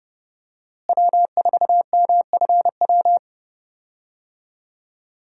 Morse code, heard as a keyed tone, about 30 wpm, 700 Hz. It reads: W4MFW
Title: USA Call Signs - 5 Characters - Repeated 2x - 30wpm